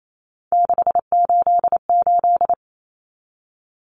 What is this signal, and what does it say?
Morse code, heard as a keyed tone, about 28 wpm, 700 Hz. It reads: 688
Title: Numbers - 3-Digit 28wpm